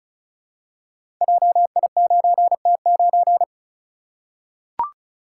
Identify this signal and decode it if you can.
Morse code, heard as a keyed tone, about 34 wpm, 700 Hz. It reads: JI9T9
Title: Code Groups - 5 Letter:Numbers - 35wpm